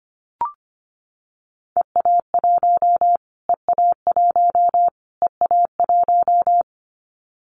Morse code, heard as a keyed tone, about 25 wpm, 700 Hz. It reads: EA1 EA1 EA1
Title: Callsign Prefixes for Top 25 DXCC - Repeated 3x - 25wpm